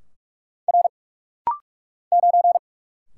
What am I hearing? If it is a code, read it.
Morse code, heard as a keyed tone, about 44 wpm, 690 Hz. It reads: R 9
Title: Single Letter-Number - Rapid-Fire 45wpm